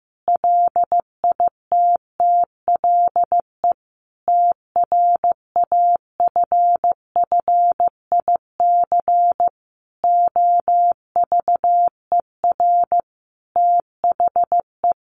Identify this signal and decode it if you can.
Morse code, heard as a keyed tone, about 15 wpm, 700 Hz. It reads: LITTLE TRAFFIC OVER THE
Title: Sentences from Top 1300 Words 15wpm